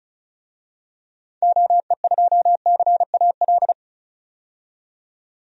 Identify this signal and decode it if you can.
Morse code, heard as a keyed tone, about 34 wpm, 700 Hz. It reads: OE2CAL
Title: DXCC-100 Call Signs - 6 Characters - 35wpm